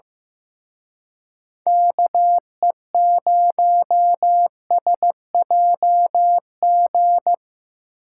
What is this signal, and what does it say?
Morse code, heard as a keyed tone, about 18 wpm, 700 Hz. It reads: KE0SJG
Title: Speed Racing - USA Call Signs 15wpm